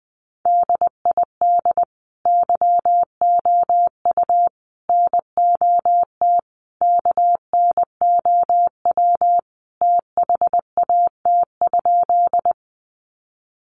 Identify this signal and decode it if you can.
Morse code, heard as a keyed tone, about 20 wpm, 700 Hz. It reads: DID YOU NOT KNOW THAT?
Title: Sentences from Top 600 Words 20wpm